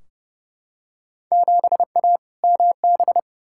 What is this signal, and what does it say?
Morse code, heard as a keyed tone, about 30 wpm, 700 Hz. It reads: 7A MB
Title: Field Day Exchanges Encore - 30wpm